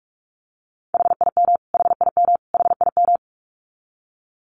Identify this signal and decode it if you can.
Morse code, heard as a keyed tone, about 44 wpm, 700 Hz. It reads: HIM HIM HIM
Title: Top 300 Words Repeated 45wpm